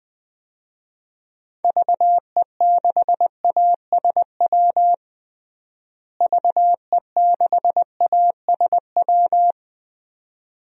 Morse code, heard as a keyed tone, about 21 wpm, 700 Hz. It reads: VE6ASW VE6ASW
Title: DXCC-100 Call Signs - 6 Characters - Repeated 2x - 20wpm